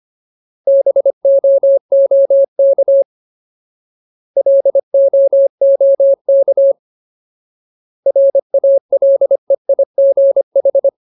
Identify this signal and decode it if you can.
Morse code, heard as a keyed tone, about 25 wpm, 550 Hz. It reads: BOOK LOOK RALEIGH
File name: Ultimate Word List / 25WPM Ultimate